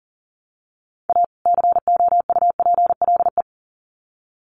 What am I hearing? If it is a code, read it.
Morse code, heard as a keyed tone, about 40 wpm, 700 Hz. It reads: A COUPLE
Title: Sets of 2 Words 40wpm